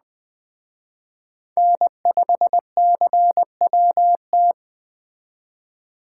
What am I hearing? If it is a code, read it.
Morse code, heard as a keyed tone, about 20 wpm, 700 Hz. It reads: N5CWT